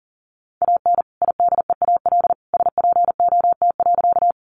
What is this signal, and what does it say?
Morse code, heard as a keyed tone, about 40 wpm, 700 Hz. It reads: AN IDEAL SPOT.
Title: Sentences from Top 1400 Words 40wpm